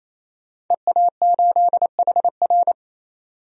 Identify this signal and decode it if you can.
Morse code, heard as a keyed tone, about 28 wpm, 700 Hz. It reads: EA8HR
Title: DXCC-100 Call Signs - 5 Characters - 28wpm